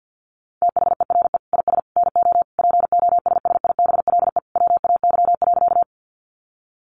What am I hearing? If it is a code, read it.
Morse code, heard as a keyed tone, about 50 wpm, 700 Hz. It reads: THERE IS NO POSSIBLE WAY.
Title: Sentences from Top 500 Words 50wpm